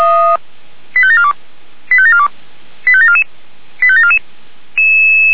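Five-tone selective call (ZVEI), heard as quick runs of five tones, tones of about 70 ms. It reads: 85742 85742 85744 85744
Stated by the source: ZVEI five-tone selective call for 85719, received by SDR.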